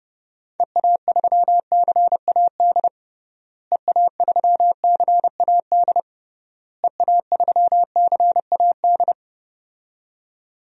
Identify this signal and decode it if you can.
Morse code, heard as a keyed tone, about 30 wpm, 700 Hz. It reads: EA3CAD EA3CAD EA3CAD
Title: DXCC-100 Call Signs - 6 Characters - Repeated 3x - 30wpm